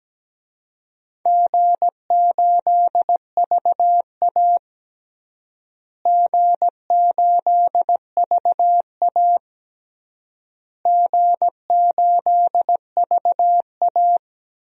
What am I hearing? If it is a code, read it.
Morse code, heard as a keyed tone, about 17 wpm, 700 Hz. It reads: G8VA G8VA G8VA
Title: DXCC-100 Call Signs - 4 Characters - Repeated 3x - 17wpm